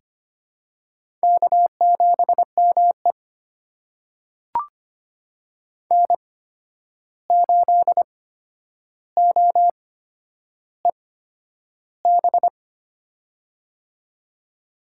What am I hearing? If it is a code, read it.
Morse code, heard as a keyed tone, about 25 wpm, spaced as for 10 wpm, 700 Hz. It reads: K7ME N8OEB
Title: USA Call Signs - 8x Character Spacing - 25wpm